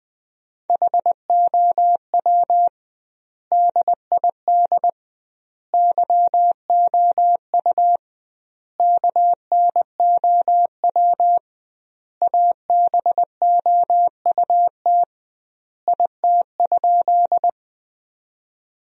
Morse code, heard as a keyed tone, about 20 wpm, 700 Hz. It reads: HOW DID YOU KNOW ABOUT IT?
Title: 2x Word-Spacing - Sentences from Top 100 Words - 20wpm